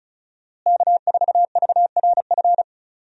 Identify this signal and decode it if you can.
Morse code, heard as a keyed tone, about 34 wpm, 700 Hz. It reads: K4VRF
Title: USA Call Signs - 5 Characters - Repeated 2x - 35wpm